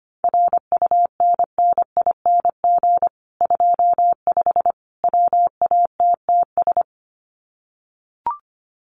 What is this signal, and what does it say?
Morse code, heard as a keyed tone, about 25 wpm, 700 Hz. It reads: RUNNING 25 WATTS
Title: Common QSO Phrases 25wpm